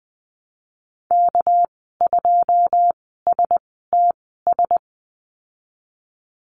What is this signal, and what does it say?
Morse code, heard as a keyed tone, about 19 wpm, spaced as for 14 wpm, 700 Hz. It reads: K2STS
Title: 213 Lesson 34g - Number 8 - Call Signs 20wpm